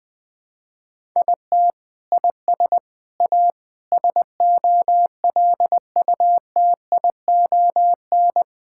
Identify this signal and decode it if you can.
Morse code, heard as a keyed tone, about 21 wpm, 700 Hz. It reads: IT IS A SOLUTION
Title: Sentences with 7-Word Phrases - 20wpm